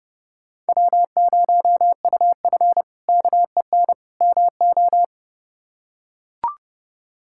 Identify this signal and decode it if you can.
Morse code, heard as a keyed tone, about 30 wpm, 700 Hz. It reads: W0UF KEN MO
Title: K1USN SST QSO Practice - 30wpm